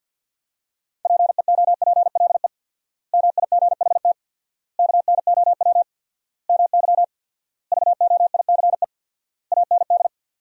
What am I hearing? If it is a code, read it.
Morse code, heard as a keyed tone, about 50 wpm, 700 Hz. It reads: PEOPLE MIGHT KNOW MY VOICE AND
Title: 4x Word-Spacing - Sentences from Top 500 Words - 50wpm